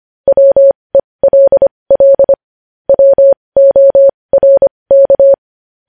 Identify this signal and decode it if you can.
Morse code, heard as a keyed tone, about 25 wpm, spaced as for 20 wpm, 550 Hz. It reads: WELL WORK
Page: news headlines in morse code at 20 wpm